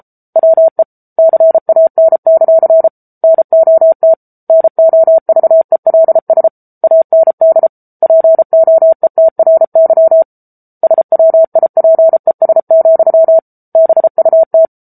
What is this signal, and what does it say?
Morse code, heard as a keyed tone, about 34 wpm, 650 Hz. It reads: WE CAN; NOT NOVELS AND POETRY SWIPES, BUT
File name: Book Two / 07-Chapter 7 - 33 WPM